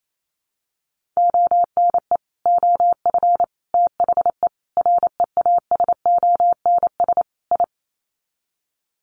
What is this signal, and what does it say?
Morse code, heard as a keyed tone, about 28 wpm, 700 Hz. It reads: ONE OF THE REASONS I
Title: Sets of 5 Words Encore2 28wpm